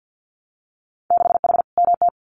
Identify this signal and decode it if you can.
Morse code, heard as a keyed tone, about 50 wpm, 700 Hz. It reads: 6H MT
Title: Winter Field Day Exchanges Encore 50wpm